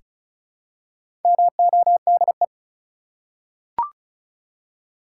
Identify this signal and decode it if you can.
Morse code, heard as a keyed tone, about 36 wpm, 700 Hz. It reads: MODE
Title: Four Letter Words - 35wpm